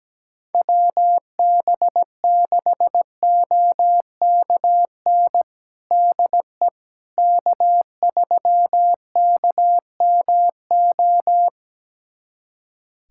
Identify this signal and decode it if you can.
Morse code, heard as a keyed tone, about 17 wpm, 700 Hz. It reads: WB6OKN DE K3KMO
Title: USA Callsign DE Callsign 17wpm